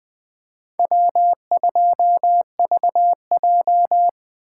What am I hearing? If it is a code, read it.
Morse code, heard as a keyed tone, about 20 wpm, 700 Hz. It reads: W2VJ
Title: USA Call Signs - 4 Characters - 2x Character Spacing - 20wpm